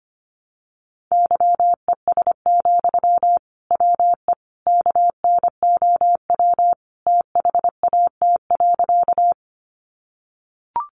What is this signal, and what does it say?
Morse code, heard as a keyed tone, about 25 wpm, 700 Hz. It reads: YES, WE KNOW THAT.
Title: Sentences from Top 500 Words 25wpm